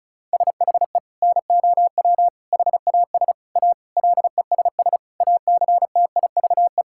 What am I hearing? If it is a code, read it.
Morse code, heard as a keyed tone, about 36 wpm, 700 Hz. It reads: SHE NOW HAS A LESS ACTIVE
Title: Sentences from Top 1000 Words - Review Part 6 - 35wpm